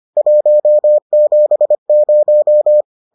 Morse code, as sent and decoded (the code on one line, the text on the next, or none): .---- --... -----
170